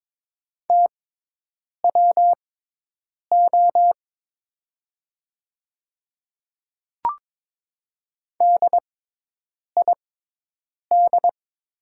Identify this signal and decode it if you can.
Morse code, heard as a keyed tone, about 22 wpm, spaced as for 8 wpm, 700 Hz. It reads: TWO DID
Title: 6x Character Spacing - Top 100 Words - 22wpm